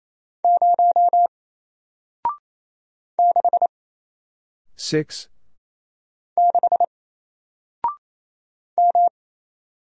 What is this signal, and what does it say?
Morse code, heard as a keyed tone, about 28 wpm, 700 Hz. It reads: 0 6 6 M
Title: Single Letter-Number Alt 28wpm